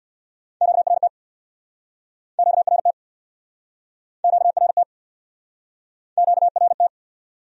Morse code, heard as a keyed tone, about 64 wpm, 700 Hz. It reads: QRT QRT QRT QRT